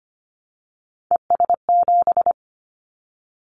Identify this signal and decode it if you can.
Morse code, heard as a keyed tone, about 25 wpm, 700 Hz. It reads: ES7
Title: Callsign Prefixes for Top 100 DXCC - 25wpm